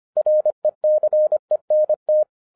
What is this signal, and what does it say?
Morse code, heard as a keyed tone, about 25 wpm, 610 Hz. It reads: RECENT
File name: 2012-01-mp3 / n14-FUN WITH NUMBERS-25wpm